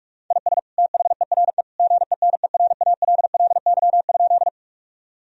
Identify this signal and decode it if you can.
Morse code, heard as a keyed tone, about 46 wpm, 700 Hz. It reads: IS THERE GENERALLY?
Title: Sentences from Top 1000 Words - Review Part 4 - 45wpm